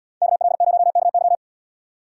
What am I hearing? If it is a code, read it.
Morse code, heard as a keyed tone, about 78 wpm, 700 Hz. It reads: KL0KY